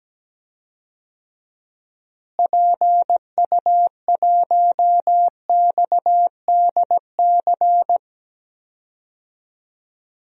Morse code, heard as a keyed tone, about 17 wpm, 700 Hz. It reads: PU1XDC
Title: DXCC-100 Call Signs - 6 Characters - Repeated 2x - 17wpm